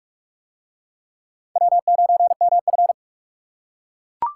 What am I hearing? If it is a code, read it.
Morse code, heard as a keyed tone, about 38 wpm, 700 Hz. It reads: W9MF